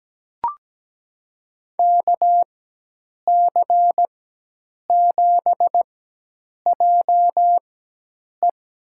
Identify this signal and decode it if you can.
Morse code, heard as a keyed tone, about 17 wpm, spaced as for 8 wpm, 700 Hz. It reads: KC7JE
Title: USA Call Signs - 4x Character Spacing - 17wpm